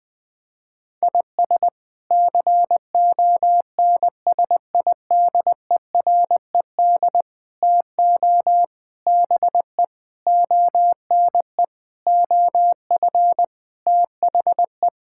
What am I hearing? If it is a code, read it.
Morse code, heard as a keyed tone, about 20 wpm, 700 Hz. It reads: IS CONSIDERED TO BE ONE OF THE
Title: Sets of 8 Words - 20wpm